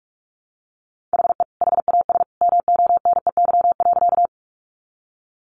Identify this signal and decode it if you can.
Morse code, heard as a keyed tone, about 44 wpm, 700 Hz. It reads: HE HAS MONEY.
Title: Sentences from Top 400 Words 45wpm